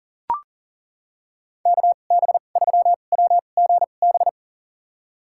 Morse code, heard as a keyed tone, about 40 wpm, spaced as for 29 wpm, 700 Hz. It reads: KB3WGB
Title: USA Call Signs - 6 Characters - 2x Character Spacing - 40wpm